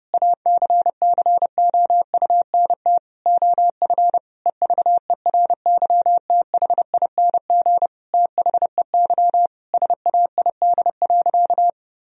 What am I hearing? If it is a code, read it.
Morse code, heard as a keyed tone, about 30 wpm, 700 Hz. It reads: ACCOUNT OF EVERYTHING THEY SAID.
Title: Sentences from Top 1000 Words - Review Part 3 - 30wpm